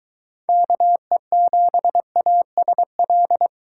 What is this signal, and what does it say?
Morse code, heard as a keyed tone, about 23 wpm, 700 Hz. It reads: KE7ASL